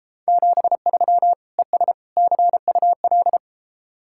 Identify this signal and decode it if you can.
Morse code, heard as a keyed tone, about 33 wpm, 700 Hz. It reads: 73 ES CUL